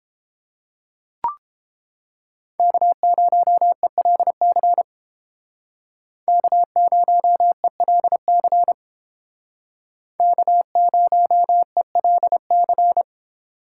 Morse code, heard as a keyed tone, about 29 wpm, 700 Hz. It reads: K0ELC K0ELC K0ELC